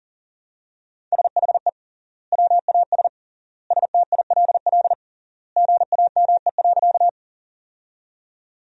Morse code, heard as a keyed tone, about 40 wpm, 700 Hz. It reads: SHE WAS STILL GAME.
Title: 3x Word-Spacing - Sentences from Top 500 Words - 40wpm